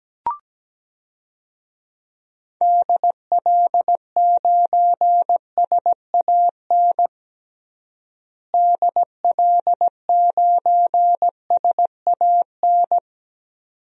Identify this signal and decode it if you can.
Morse code, heard as a keyed tone, about 17 wpm, 700 Hz. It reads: DL9SAN DL9SAN
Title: DXCC-100 Call Signs - 6 Characters - Repeated 2x - 17wpm